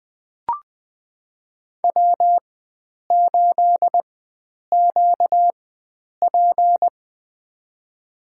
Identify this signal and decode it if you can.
Morse code, heard as a keyed tone, about 20 wpm, spaced as for 9 wpm, 700 Hz. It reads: W8QP
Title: USA Call Signs - 4 Characters - 4x Character Spacing - 20wpm